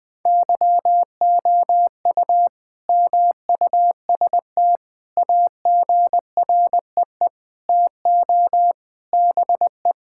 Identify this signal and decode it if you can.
Morse code, heard as a keyed tone, about 20 wpm, 700 Hz. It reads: YOU MUST AGREE TO BE
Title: Sentences with 9-Word Phrases - 20wpm